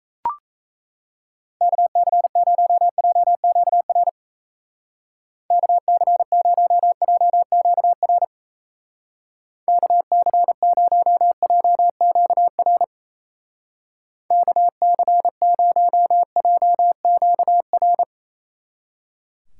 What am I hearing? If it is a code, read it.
Morse code, heard as a keyed tone, about 35 wpm, 700 Hz. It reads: KC0JQR KC0JQR KC0JQR KC0JQR
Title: Speed Racing - USA 2x3 Call Signs 28wpm